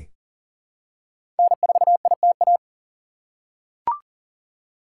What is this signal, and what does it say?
Morse code, heard as a keyed tone, about 38 wpm, 700 Hz. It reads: N4ITA
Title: USA Call Signs - 5 Characters - 3x Character Spacing - 40wpm